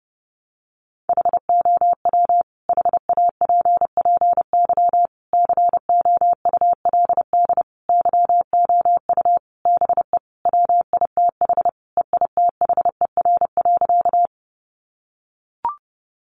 Morse code, heard as a keyed tone, about 30 wpm, 700 Hz. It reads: HOW HAPPY COULD YOU BE WITH EITHER.
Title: Sentences from Top 1000 Words 30wpm